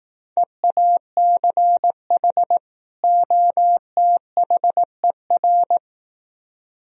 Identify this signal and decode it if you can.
Morse code, heard as a keyed tone, about 18 wpm, 700 Hz. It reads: EACH OTHER
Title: Speed-Racing - Sets of 2 Words 15wpm